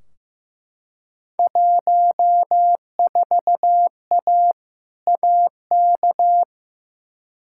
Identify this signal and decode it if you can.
Morse code, heard as a keyed tone, about 15 wpm, 700 Hz. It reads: 14A AK
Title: Field Day Exchanges Encore - 15wpm